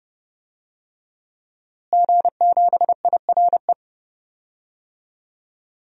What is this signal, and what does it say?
Morse code, heard as a keyed tone, about 32 wpm, 700 Hz. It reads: G7IRE
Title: DXCC-100 Call Signs - 5 Characters - Repeated 2x - 30wpm